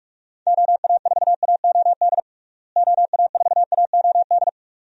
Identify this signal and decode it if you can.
Morse code, heard as a keyed tone, about 41 wpm, 700 Hz. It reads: OA4AOD OA4AOD
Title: USA Call Signs - 6 Characters - Repeated 2x - 45wpm